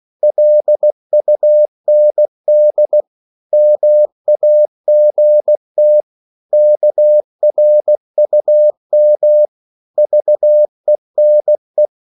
Morse code, heard as a keyed tone, about 16 wpm, 600 Hz. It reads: LUND MAGT KRUM VENE
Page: Korte danske ord 16wpm